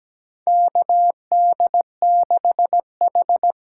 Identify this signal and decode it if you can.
Morse code, heard as a keyed tone, about 17 wpm, 700 Hz. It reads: KD6H
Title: USA Call Signs - 6 Characters - 8x Character Spacing - 17wpm